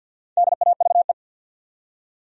Morse code, heard as a keyed tone, about 48 wpm, 700 Hz. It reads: DAVE
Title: USA Common Names - 50wpm